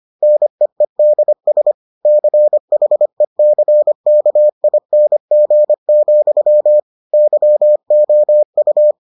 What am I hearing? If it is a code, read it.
Morse code, heard as a keyed tone, about 25 wpm, 600 Hz. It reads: NEEDS CHECKING, YOU